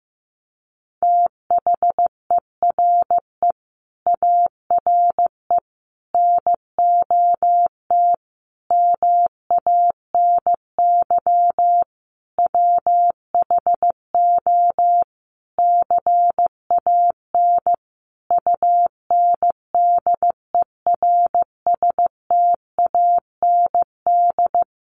Morse code, heard as a keyed tone, about 15 wpm, 700 Hz. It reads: THERE ARE NOT MANY WHO CAN UNDERSTAND
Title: Sentences from Top 1300 Words 15wpm